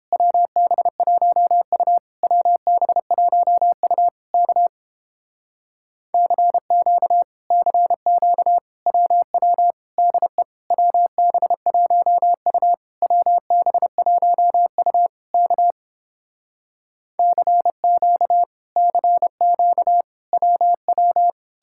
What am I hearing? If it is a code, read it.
Morse code, heard as a keyed tone, about 31 wpm, 700 Hz. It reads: WB1U WB1U K CQ CQ WW DE WB1U WB1U K CQ CQ WW